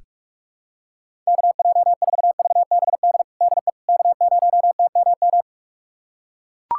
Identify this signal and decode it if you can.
Morse code, heard as a keyed tone, about 41 wpm, 690 Hz. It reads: KJ4VBD DE K0TMM
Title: USA Callsign DE Callsign 45wpm